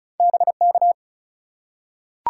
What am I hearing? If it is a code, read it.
Morse code, heard as a keyed tone, about 33 wpm, 700 Hz. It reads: BK
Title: Top 180 QSO Elements 35wpm